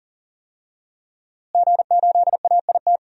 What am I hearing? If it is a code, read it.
Morse code, heard as a keyed tone, about 40 wpm, 700 Hz. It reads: G8AIT
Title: DXCC-100 Call Signs - 5 Characters - Repeated 3x - 40wpm